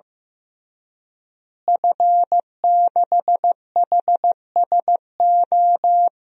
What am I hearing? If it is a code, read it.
Morse code, heard as a keyed tone, about 15 wpm, 700 Hz. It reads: F6HSO
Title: DXCC-100 Call Signs - 5 Characters - Repeated 3x - 15wpm